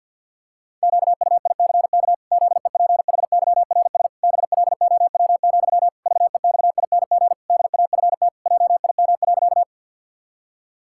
Morse code, heard as a keyed tone, about 50 wpm, 700 Hz. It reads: QUICK ZEPHYRS BLOW, VEXING DAFT JIM.
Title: Pangram Challenge v2 - 50wpm